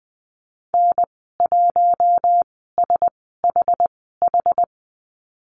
Morse code, heard as a keyed tone, about 20 wpm, spaced as for 14 wpm, 700 Hz. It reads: N1SHH
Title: 062 Lesson 11g - Letter h - Call Signs 20wpm